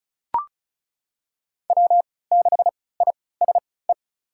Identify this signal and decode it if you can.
Morse code, heard as a keyed tone, about 36 wpm, spaced as for 20 wpm, 700 Hz. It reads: W6ISE
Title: USA Call Signs - 5 Characters - 3x Character Spacing - 35wpm